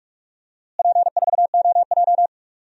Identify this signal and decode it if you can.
Morse code, heard as a keyed tone, about 46 wpm, 700 Hz. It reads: P4OJ
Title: DXCC-100 Call Signs - 4 Characters - Repeated 2x - 45wpm